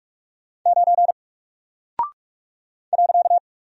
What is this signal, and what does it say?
Morse code, heard as a keyed tone, about 44 wpm, 700 Hz. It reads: 9 .
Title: Single Character (Letter:Number:Punctuation:Prosign) Alt 45wpm